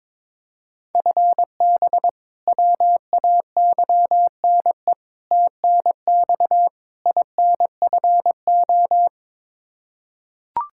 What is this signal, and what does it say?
Morse code, heard as a keyed tone, about 22 wpm, 700 Hz. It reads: FB WAYNE TNX INFO
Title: K1USN SST QSO Practice 22wpm